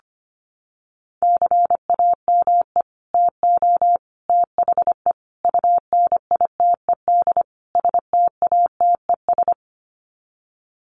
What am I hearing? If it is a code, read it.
Morse code, heard as a keyed tone, about 25 wpm, 700 Hz. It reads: CAME TO THE UNITED STATES
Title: Sets of 5 Words Encore2 25wpm